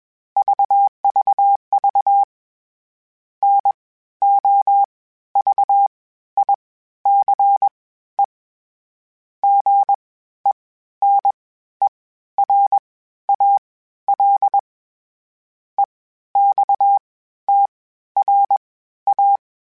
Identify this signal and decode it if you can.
Morse code, heard as a keyed tone, about 21 wpm, spaced as for 13 wpm, 800 Hz. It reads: VVV NOVICE GENERAL EXTRA
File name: (spaced) / FromSMs spaced mp3s / Classes sp@20